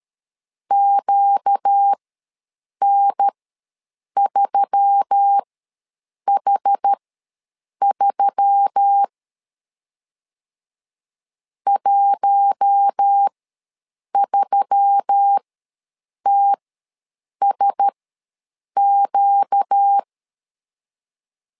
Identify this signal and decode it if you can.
Morse code, heard as a keyed tone, about 13 wpm, spaced as for 7 wpm, 800 Hz. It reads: QN3H3 13TSQ
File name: LEZIONI / 4 Q1S3 ws 12wpm 2038